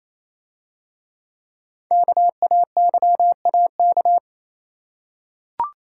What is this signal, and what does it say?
Morse code, heard as a keyed tone, about 27 wpm, 700 Hz. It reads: KAYAK
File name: Easy Words - Speed-Racing with Hints - 22wpm